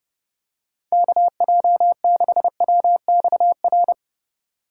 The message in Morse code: KJ6WXR

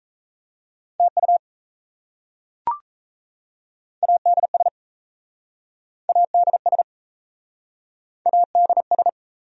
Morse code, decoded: TU ADS ADS ADS

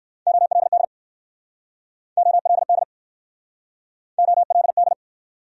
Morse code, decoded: OLD OLD OLD

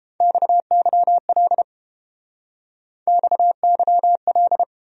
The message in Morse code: XYL XYL

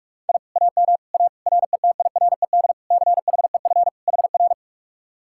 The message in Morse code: I AM A RETIRED CHEF HR

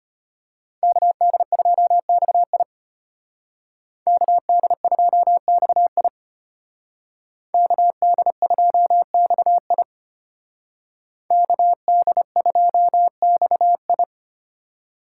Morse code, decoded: KD2XI KD2XI KD2XI KD2XI

